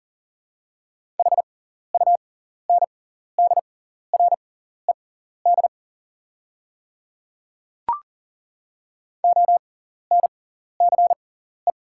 Morse code, heard as a keyed tone, about 40 wpm, spaced as for 14 wpm, 690 Hz. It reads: HUNDRED ONCE
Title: 6x Character Spacing - Top 500 Words - 40wpm